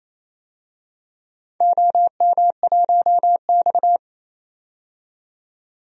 Morse code, heard as a keyed tone, about 28 wpm, 700 Hz. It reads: OM1X
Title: DXCC-100 Call Signs - 4 Characters - 28wpm